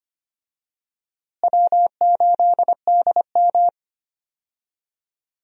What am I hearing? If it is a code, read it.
Morse code, heard as a keyed tone, about 25 wpm, 700 Hz. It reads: W8DM
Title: USA Call Signs - 4 Characters - Repeated 3x - 25wpm